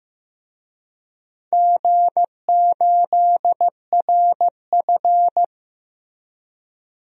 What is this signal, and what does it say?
Morse code, heard as a keyed tone, about 16 wpm, 700 Hz. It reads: G8RF
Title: DXCC-100 Call Signs - 4 Characters - Repeated 3x - 15wpm